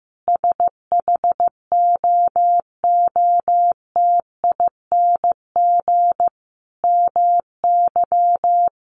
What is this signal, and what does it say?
Morse code, heard as a keyed tone, about 15 wpm, 700 Hz. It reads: SHOOTING MY
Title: A Sound Of Thunder 15wpm